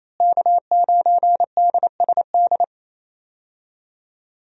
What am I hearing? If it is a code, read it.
Morse code, heard as a keyed tone, about 28 wpm, 700 Hz. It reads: K9DSD
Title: USA Call Signs - 5 Characters - Repeated 3x - 28wpm